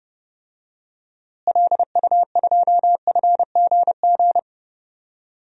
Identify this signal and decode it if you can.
Morse code, heard as a keyed tone, about 30 wpm, 700 Hz. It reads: LU2FGG
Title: DXCC-100 Call Signs - 6 Characters - Repeated 2x - 30wpm